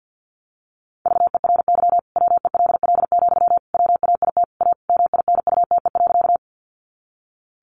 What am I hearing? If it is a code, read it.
Morse code, heard as a keyed tone, about 50 wpm, 700 Hz. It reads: VERY WELL, WAIT A MINUTE.
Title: Sentences from Top 500 Words 50wpm